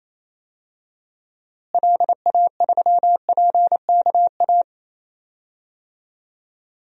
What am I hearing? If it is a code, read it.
Morse code, heard as a keyed tone, about 28 wpm, 700 Hz. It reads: LA3PKA
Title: DXCC-100 Call Signs - 6 Characters - Repeated 2x - 28wpm